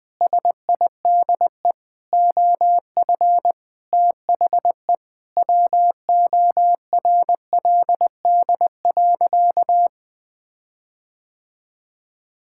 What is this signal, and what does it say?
Morse code, heard as a keyed tone, about 21 wpm, 700 Hz. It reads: SIDE OF THE WORLD.
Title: Sentences with 7-Word Phrases - 20wpm